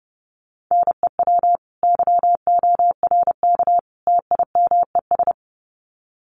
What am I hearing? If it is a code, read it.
Morse code, heard as a keyed tone, about 30 wpm, 700 Hz. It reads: NEW YORK TIMES
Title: Sets of 3 Words 30wpm